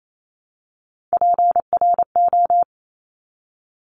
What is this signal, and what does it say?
Morse code, heard as a keyed tone, about 28 wpm, 700 Hz. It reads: PRO
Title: Top Three Letter Combinations in Words 28wpm